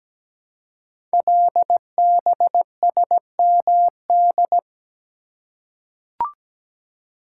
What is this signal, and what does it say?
Morse code, heard as a keyed tone, about 17 wpm, 700 Hz. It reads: LBSMD
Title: Code Groups - 5 Letters - 17wpm